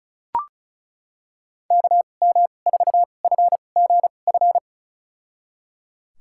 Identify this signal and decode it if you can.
Morse code, heard as a keyed tone, about 34 wpm, spaced as for 25 wpm, 690 Hz. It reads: KM4FGF
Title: USA Call Signs - 6 Characters - 2x Character Spacing - 35wpm